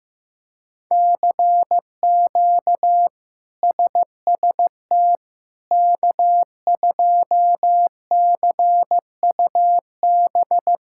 Morse code, heard as a keyed tone, about 16 wpm, 700 Hz. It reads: CQ SST K2CUB
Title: K1USN SST QSO Practice - 15wpm